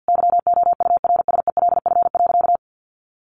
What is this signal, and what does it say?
Morse code, heard as a keyed tone, about 48 wpm, 700 Hz. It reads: YOURSELF.
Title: Sentences from Top 1400 Words 50wpm